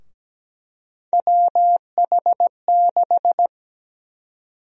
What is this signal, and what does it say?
Morse code, heard as a keyed tone, about 17 wpm, 700 Hz. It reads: WH6
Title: USA Call Sign Prefixes - Repeated 2x - 17wpm